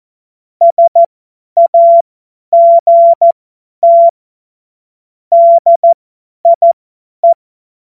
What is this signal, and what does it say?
Morse code, heard as a keyed tone, about 14 wpm, spaced as for 10 wpm, 680 Hz. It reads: SAGT DIE
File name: Morsekurs / Witze / Witz 12. Oktober 2021, 50 F70